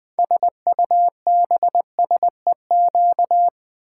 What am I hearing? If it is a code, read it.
Morse code, heard as a keyed tone, about 20 wpm, 700 Hz. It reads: SUBSEQ
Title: Build A Word - 6 Letters At A Time - 12-Letter Words - 20wpm